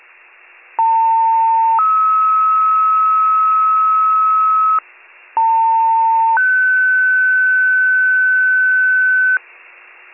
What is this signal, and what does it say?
Two-tone page, heard as a one-second tone, then a three-second tone, about 900 then 1290 Hz, about 900 then 1500 Hz, with noise. It is heard from 0.8 to 9.4 s.